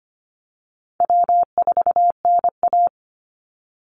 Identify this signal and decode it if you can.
Morse code, heard as a keyed tone, about 25 wpm, 700 Hz. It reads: W4NA
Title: Easy Call Signs - 25wpm